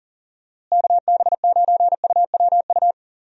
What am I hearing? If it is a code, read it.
Morse code, heard as a keyed tone, about 40 wpm, 700 Hz. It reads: KB9UWU
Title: USA Call Signs - 6 Characters - Repeated 2x - 40wpm